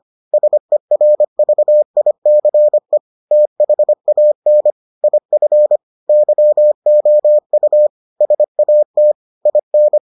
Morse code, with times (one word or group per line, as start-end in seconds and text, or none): 0.3-3.0 SERVICE
3.3-4.7 THAN
5.0-5.8 IF
6.1-7.9 YOU
8.2-9.1 SAT
9.5-10.0 IN